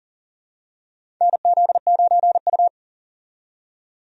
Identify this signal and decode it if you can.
Morse code, heard as a keyed tone, about 37 wpm, 700 Hz. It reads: NZ9U
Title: USA Call Signs - 4 Characters - 40wpm